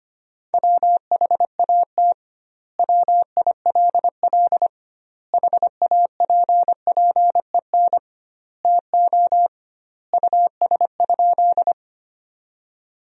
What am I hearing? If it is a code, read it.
Morse code, heard as a keyed tone, about 25 wpm, 700 Hz. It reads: WHAT WILL HAPPEN TO US?
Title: 2x Word-Spacing - Sentences from Top 400 Words - 25wpm